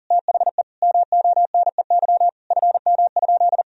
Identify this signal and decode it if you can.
Morse code, heard as a keyed tone, about 40 wpm, 700 Hz. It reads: THE MONEY FM?
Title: Sentences from Top 400 Words and Abbreviations 40wpm